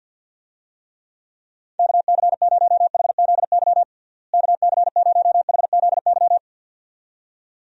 Morse code, heard as a keyed tone, about 46 wpm, 700 Hz. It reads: KC0HZY KC0HZY
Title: USA Call Signs - 6 Characters - Repeated 2x - 50wpm